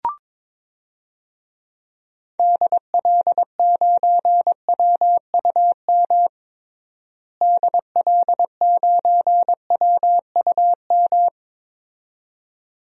Morse code, heard as a keyed tone, about 22 wpm, 700 Hz. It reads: DL9WUM DL9WUM
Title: DXCC-100 Call Signs - 6 Characters - Repeated 2x - 22wpm